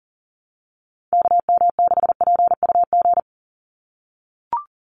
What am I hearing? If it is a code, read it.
Morse code, heard as a keyed tone, about 37 wpm, 690 Hz. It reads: KM6PUG
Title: USA Call Signs 40wpm